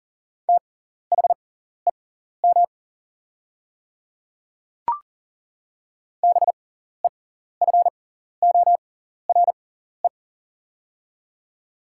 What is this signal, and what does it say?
Morse code, heard as a keyed tone, about 37 wpm, spaced as for 14 wpm, 700 Hz. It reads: THEM BEFORE
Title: 6x Character Spacing - Top 200 Words - 40wpm